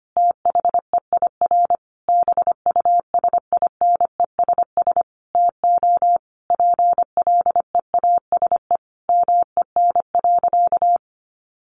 Morse code, heard as a keyed tone, about 25 wpm, 700 Hz. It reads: THEIR BUSINESS TO PLEASE MEN.
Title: Sentences from Top 1500 Words 25wpm